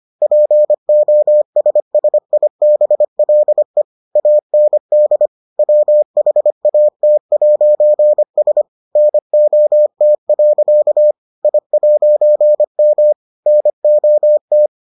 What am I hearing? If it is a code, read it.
Morse code, heard as a keyed tone, about 25 wpm, 600 Hz. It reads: POSSIBLE AND WHAT'S NOT. I'M NOT